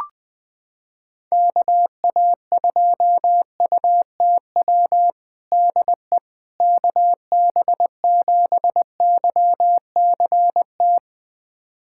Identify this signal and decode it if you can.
Morse code, heard as a keyed tone, about 21 wpm, 700 Hz. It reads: KA2UTW DE KB7YCT
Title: USA Callsign DE Callsign 20wpm